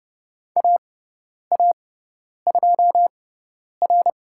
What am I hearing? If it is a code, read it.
Morse code, heard as a keyed tone, about 30 wpm, spaced as for 10 wpm, 700 Hz. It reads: AA2R
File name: Lesson 17g - Letter u - Call Signs 30wpm